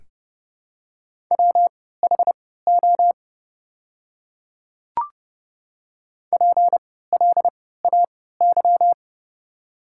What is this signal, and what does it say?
Morse code, heard as a keyed tone, about 30 wpm, spaced as for 17 wpm, 700 Hz. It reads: WHO PLAY
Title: 3x Character Spacing - Top 200 Words - 30wpm